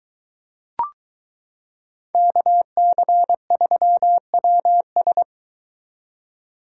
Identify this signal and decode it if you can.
Morse code, heard as a keyed tone, about 22 wpm, 700 Hz. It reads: KC3WS